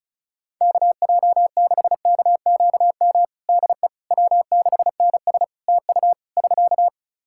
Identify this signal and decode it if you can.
Morse code, heard as a keyed tone, about 34 wpm, 700 Hz. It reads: KJ6KQM DE W6NS TU <SK>
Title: Practice QSO - Exchange 4 - Ending - 35wpm